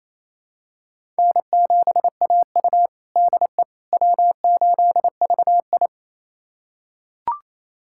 Morse code, heard as a keyed tone, about 28 wpm, 700 Hz. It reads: N7AU DE W8VI
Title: USA Callsign DE Callsign - 4 Characters - 28wpm